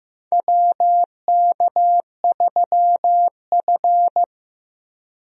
Morse code, heard as a keyed tone, about 15 wpm, 700 Hz. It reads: WK3F
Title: USA Call Signs - 4 Characters - Repeated 2x - 15wpm